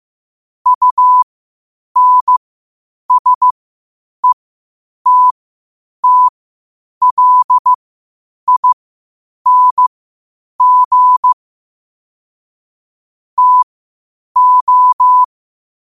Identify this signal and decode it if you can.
Morse code, heard as a keyed tone, about 15 wpm, spaced as for 8 wpm, 1000 Hz. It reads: UNSETTLING TO
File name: qotdfiles / 8wpm